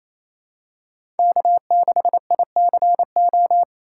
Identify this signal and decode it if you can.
Morse code, heard as a keyed tone, about 28 wpm, 700 Hz. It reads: K6ICO
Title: USA Call Signs - 5 Characters - 6x Character Spacing - 28wpm